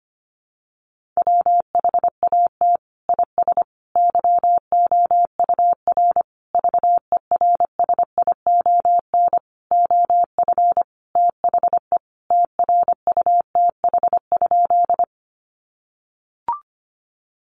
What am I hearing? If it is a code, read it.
Morse code, heard as a keyed tone, about 25 wpm, 700 Hz. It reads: WHAT IS YOUR VERSION OF THE TRUTH?
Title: Sentences from Top 1400 Words 25wpm